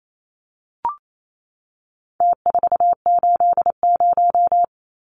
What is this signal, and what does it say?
Morse code, heard as a keyed tone, about 28 wpm, 700 Hz. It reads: T480
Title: Numbers - Contest Style - 4-Digit 28wpm